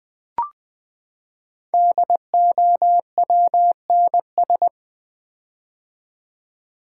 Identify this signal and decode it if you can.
Morse code, heard as a keyed tone, about 20 wpm, 700 Hz. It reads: DOWNS
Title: Build A Word - 5 Letters At A Time - 10-Letter Words - 20wpm